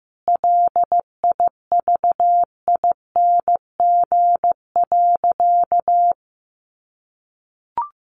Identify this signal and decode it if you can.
Morse code, heard as a keyed tone, about 15 wpm, 700 Hz. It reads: LIVING.
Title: Sentences from Top 1500 Words 15wpm